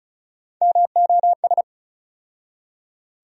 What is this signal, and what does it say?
Morse code, heard as a keyed tone, about 35 wpm, 700 Hz. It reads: MOS
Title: USA Call Sign Postfixes - 3 Letters 35wpm